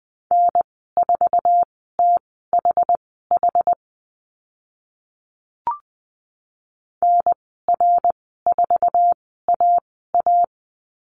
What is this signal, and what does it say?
Morse code, heard as a keyed tone, about 20 wpm, spaced as for 14 wpm, 700 Hz. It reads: N4THH NR4AA
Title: 062 Lesson 11g - Letter h - Call Signs 20wpm